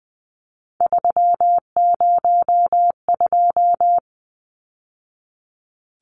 Morse code, heard as a keyed tone, about 19 wpm, 700 Hz. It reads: 302
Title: Numbers - 3-Digit 20wpm